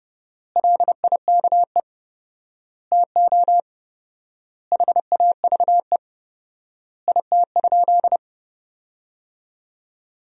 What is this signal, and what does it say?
Morse code, heard as a keyed tone, about 30 wpm, 700 Hz. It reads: LIKE TO HAVE IT?
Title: 4x Word-Spacing - Sentences from Top 100 Words - 30wpm